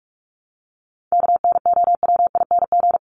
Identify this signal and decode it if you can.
Morse code, heard as a keyed tone, about 44 wpm, 700 Hz. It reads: KNOWING
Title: ING Suffix - Top 100 Words with Base Words 45wpm